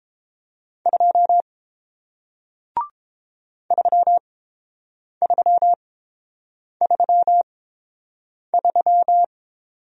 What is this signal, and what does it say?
Morse code, heard as a keyed tone, about 30 wpm, spaced as for 7 wpm, 700 Hz. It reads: 2 3333